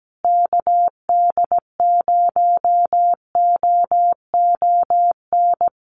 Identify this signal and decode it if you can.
Morse code, heard as a keyed tone, about 17 wpm, 700 Hz. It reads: KD0OON
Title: USA Call Signs Repeated 2x 17wpm